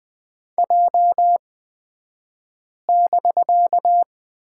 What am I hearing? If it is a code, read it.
Morse code, heard as a keyed tone, about 20 wpm, spaced as for 5 wpm, 700 Hz. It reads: J<BK>
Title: 249 Lesson 40d - BK - Two Characters 20wpm